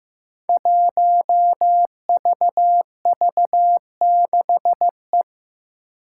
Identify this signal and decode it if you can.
Morse code, heard as a keyed tone, about 15 wpm, 700 Hz. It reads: 1VV6E
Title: Five Challenging Letter-Number Combinations 15wpm